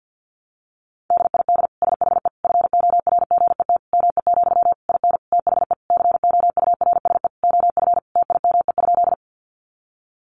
Morse code, heard as a keyed tone, about 50 wpm, 700 Hz. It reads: DID SHE FORGET ME, IN THE COURSE OF TIME?
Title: Sentences from Top 1200 Words 50wpm